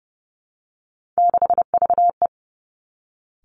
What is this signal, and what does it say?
Morse code, heard as a keyed tone, about 30 wpm, 700 Hz. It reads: 6VE
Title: Three Challenging Letter-Number Combinations 30wpm